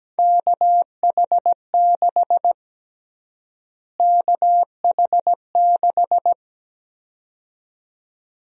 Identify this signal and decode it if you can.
Morse code, heard as a keyed tone, about 17 wpm, 700 Hz. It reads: KH6 KH6
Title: USA Call Sign Prefixes - Repeated 2x - 17wpm